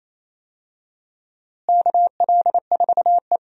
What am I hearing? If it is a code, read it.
Morse code, heard as a keyed tone, about 28 wpm, 700 Hz. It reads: KL4E
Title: USA Call Signs - 4 Characters - Repeated 2x - 28wpm